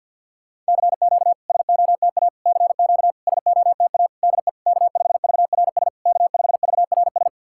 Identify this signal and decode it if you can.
Morse code, heard as a keyed tone, about 50 wpm, 690 Hz. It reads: CQ SOTA CQ SOTA DE K5VRS K5VRS
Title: SOTA QSO Practice 50wpm